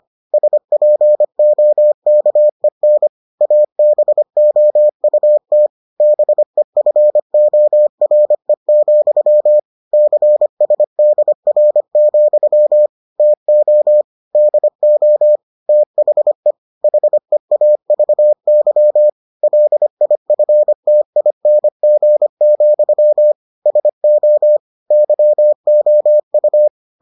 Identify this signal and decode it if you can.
Morse code, heard as a keyed tone, about 25 wpm, 600 Hz. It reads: SPOKEN ABOUT BEFORE, CSDR, TO DO THE HEAVY LIFTING, SO YOU